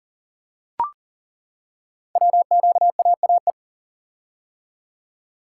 Morse code, heard as a keyed tone, about 39 wpm, 700 Hz. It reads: WQAAE
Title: Code Groups - 5 Letter:Numbers - 40wpm